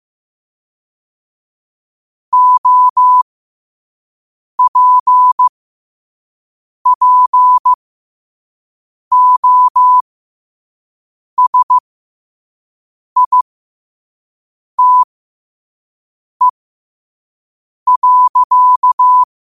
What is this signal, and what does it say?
Morse code, heard as a keyed tone, about 15 wpm, spaced as for 5 wpm, 1000 Hz. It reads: OPPOSITE.
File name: qotdfiles / 5wpm